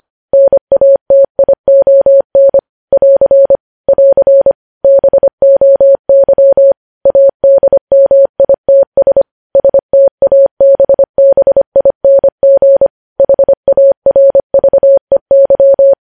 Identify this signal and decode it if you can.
Morse code, heard as a keyed tone, about 25 wpm, 550 Hz. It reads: NATION <AR> <AR> BOY ADMITS STABBING HARVEY
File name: podcast / News headlines-1x-25x25